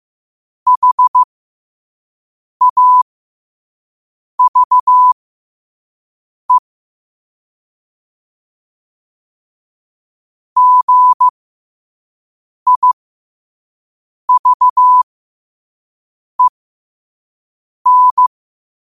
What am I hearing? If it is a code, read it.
Morse code, heard as a keyed tone, about 15 wpm, spaced as for 5 wpm, 1000 Hz. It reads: HAVE GIVEN